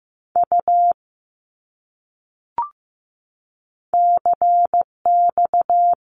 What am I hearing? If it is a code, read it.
Morse code, heard as a keyed tone, about 15 wpm, 700 Hz. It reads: U CX
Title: Abbreviations Repeated 3x - 15wpm